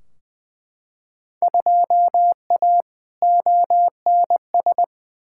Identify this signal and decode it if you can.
Morse code, heard as a keyed tone, about 20 wpm, 700 Hz. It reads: 2A ONS
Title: Field Day Exchanges Encore - 20wpm